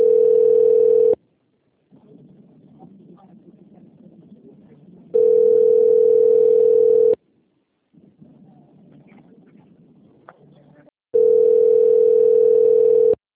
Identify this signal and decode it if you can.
Ringback tone (North America), 440 + 480 Hz.
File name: out- - - - - .21869